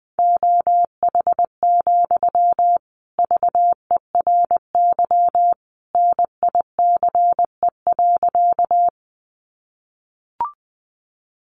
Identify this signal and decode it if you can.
Morse code, heard as a keyed tone, about 20 wpm, 700 Hz. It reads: OH, VERY NICE.
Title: Sentences from Top 1100 Words 20wpm